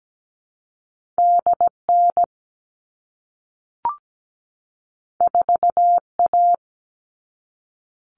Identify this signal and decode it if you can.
Morse code, heard as a keyed tone, about 17 wpm, 700 Hz. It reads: DN 4A
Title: All Cut Numbers - 2-Digit 17wpm